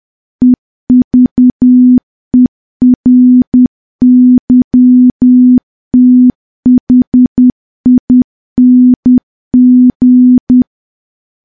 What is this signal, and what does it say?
Morse code, heard as a keyed tone, about 10 wpm, 250 Hz. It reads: EVERYTHING
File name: morse / morse-training-10WPM